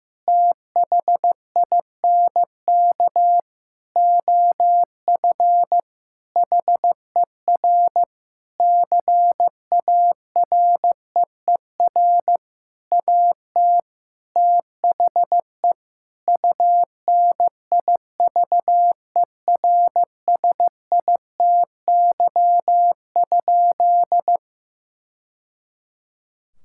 Morse code, heard as a keyed tone, about 15 wpm, 700 Hz. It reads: THINK OF HER CAREER AT THE UNIVERSITY?
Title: Sentences from Top 1000 Words - Review Part 6 - 15wpm